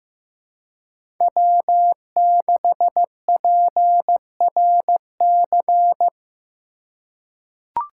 Morse code, heard as a keyed tone, about 16 wpm, 700 Hz. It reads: W6PRC
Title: USA Call Signs - 5 Characters - 4x Character Spacing - 15wpm